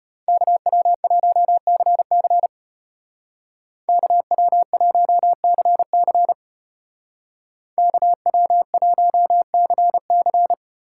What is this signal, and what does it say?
Morse code, heard as a keyed tone, about 32 wpm, 700 Hz. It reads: KW1CC KW1CC KW1CC